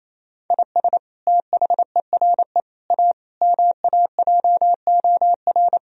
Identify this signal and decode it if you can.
Morse code, heard as a keyed tone, about 28 wpm, 700 Hz. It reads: IS THERE A MAJOR
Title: Sentences from Top 1000 Words - Review Part 4 - 28wpm